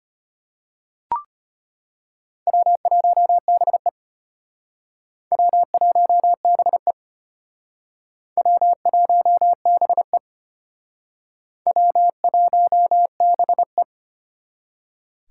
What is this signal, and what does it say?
Morse code, heard as a keyed tone, about 34 wpm, 700 Hz. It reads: W1BE W1BE W1BE W1BE